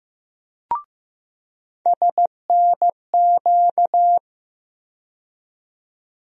Morse code, heard as a keyed tone, about 15 wpm, 700 Hz. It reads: SNQ
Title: USA Call Sign Postfixes - 3 Letters 15wpm